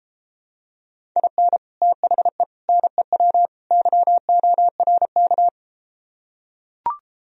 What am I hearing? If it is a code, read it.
Morse code, heard as a keyed tone, about 33 wpm, 700 Hz. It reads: IN THE NEW YORK